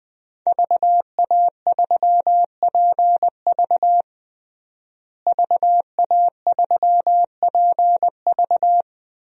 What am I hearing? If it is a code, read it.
Morse code, heard as a keyed tone, about 20 wpm, 700 Hz. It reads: VA3PV VA3PV
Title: DXCC-100 Call Signs - 5 Characters - Repeated 3x - 20wpm